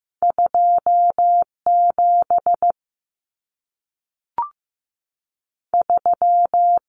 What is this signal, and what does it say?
Morse code, heard as a keyed tone, about 15 wpm, spaced as for 3 wpm, 700 Hz. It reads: 2 73
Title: Numbers - 2-Digit 15wpm